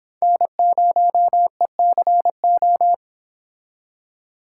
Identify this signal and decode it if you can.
Morse code, heard as a keyed tone, about 25 wpm, 700 Hz. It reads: N0ECO